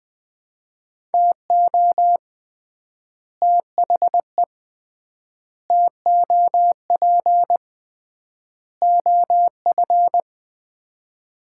Morse code, heard as a keyed tone, about 20 wpm, 700 Hz. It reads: TO THE TOP OF
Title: 3x Word-Spacing - Sets of 4 Words - 20wpm